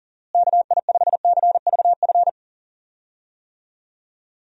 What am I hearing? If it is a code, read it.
Morse code, heard as a keyed tone, about 40 wpm, 700 Hz. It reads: KI5CVF
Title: USA Call Signs - 6 Characters - Repeated 2x - 40wpm